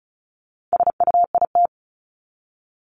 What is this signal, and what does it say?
Morse code, heard as a keyed tone, about 34 wpm, 700 Hz. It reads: SUIT
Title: Top 1501-1750 Words 35wpm